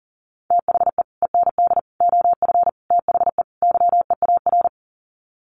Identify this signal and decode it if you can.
Morse code, heard as a keyed tone, about 40 wpm, 700 Hz. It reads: THE END OF THE YEAR
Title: Sets of 5 Words 40wpm